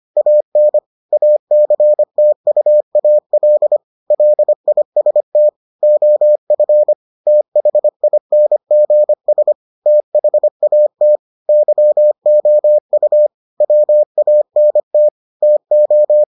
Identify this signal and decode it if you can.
Morse code, heard as a keyed tone, about 25 wpm, 600 Hz. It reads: AN ACTUAL LIST OF THINGS THAT YOU WANT TO